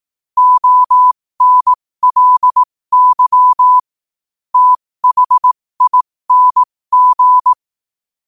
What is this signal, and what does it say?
Morse code, heard as a keyed tone, about 18 wpm, spaced as for 15 wpm, 1000 Hz. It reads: ONLY THING
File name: qotdfiles / 15wpm